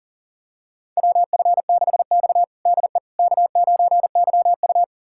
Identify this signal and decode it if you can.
Morse code, heard as a keyed tone, about 40 wpm, 700 Hz. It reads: WF6X DE K9YU
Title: USA Callsign DE Callsign - 4 Characters - 40wpm